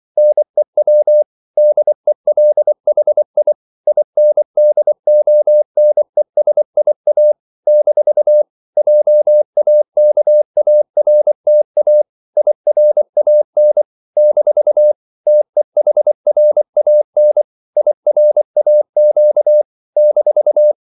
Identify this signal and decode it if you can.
Morse code, heard as a keyed tone, about 24 wpm, 600 Hz. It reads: NEW DELHI INDONESIA - JAKARTA IRAN - TEHRAN IRAQ -
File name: CW / Practice / NationalCapitals24